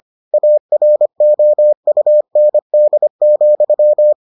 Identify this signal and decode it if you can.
Morse code, heard as a keyed tone, about 25 wpm, 600 Hz. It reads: AROUND,